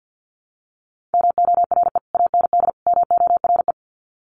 Morse code, heard as a keyed tone, about 50 wpm, 700 Hz. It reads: MORE AND MORE
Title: Sets of 3 Words 50wpm